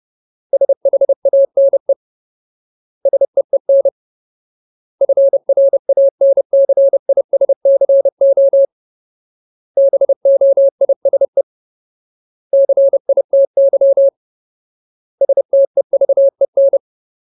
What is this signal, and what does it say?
Morse code, heard as a keyed tone, about 30 wpm, 550 Hz. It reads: SHANE SEEN FRANCISCO BOISE CITY STEVEN
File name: Ultimate Word List / 30WPM Ultimate